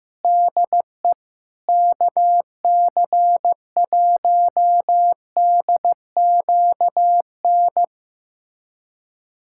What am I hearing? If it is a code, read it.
Morse code, heard as a keyed tone, about 15 wpm, 700 Hz. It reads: DE KC1DQN
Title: USA Callsign DE Callsign 15wpm